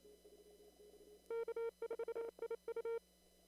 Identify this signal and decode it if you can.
Morse code, heard as a keyed tone, about 27 wpm, 460 Hz. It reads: K4IU
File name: CW / kt5j wpx cw-op-k5pi